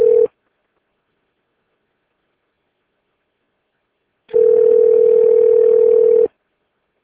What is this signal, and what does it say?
Ringback tone (North America), 440 + 480 Hz.